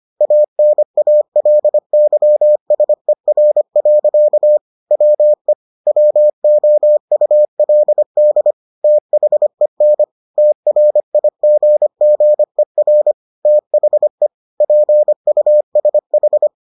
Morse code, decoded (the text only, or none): ANALYSER. WE WOULD THEN TRIGGER THE PUSH